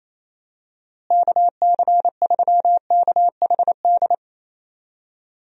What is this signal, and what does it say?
Morse code, heard as a keyed tone, about 28 wpm, 700 Hz. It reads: KC3KHD
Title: USA Call Signs - 6 Characters - Repeated 2x - 28wpm